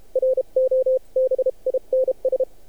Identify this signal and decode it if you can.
Morse code, heard as a keyed tone, about 30 wpm, 520 Hz. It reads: ROBINS
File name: archived / CRSnet